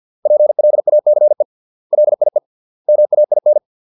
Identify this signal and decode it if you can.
Morse code, heard as a keyed tone, about 50 wpm, 600 Hz. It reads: PLACE LIE MAIN